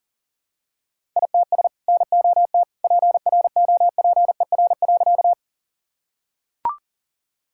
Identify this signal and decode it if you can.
Morse code, heard as a keyed tone, about 40 wpm, 700 Hz. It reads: ITS NOT PROPER.
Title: Sentences from Top 1000 Words - Review Part 5 - 40wpm